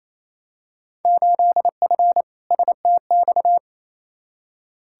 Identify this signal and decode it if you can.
Morse code, heard as a keyed tone, about 27 wpm, 700 Hz. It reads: 8F STX
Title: Field Day Exchanges Encore - 28wpm